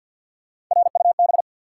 Morse code, heard as a keyed tone, about 48 wpm, 700 Hz. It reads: RUB
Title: Three Letter Words - 50wpm